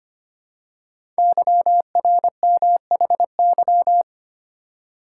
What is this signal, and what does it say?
Morse code, heard as a keyed tone, about 25 wpm, 700 Hz. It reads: YRMHY
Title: Code Groups - 5 Letters - 25wpm